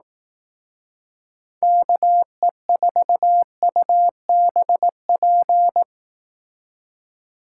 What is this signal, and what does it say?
Morse code, heard as a keyed tone, about 18 wpm, 700 Hz. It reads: KE4UBP
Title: Speed Racing - USA 2x3 Call Signs 15wpm